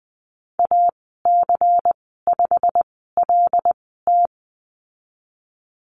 Morse code, heard as a keyed tone, about 20 wpm, spaced as for 15 wpm, 700 Hz. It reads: AC5LT
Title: 128 Lesson 21g - Number 6 - Call Signs 20wpm